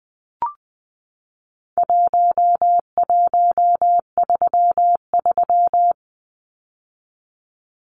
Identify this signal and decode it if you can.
Morse code, heard as a keyed tone, about 20 wpm, 700 Hz. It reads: 1133
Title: Numbers - 4-Digit 20wpm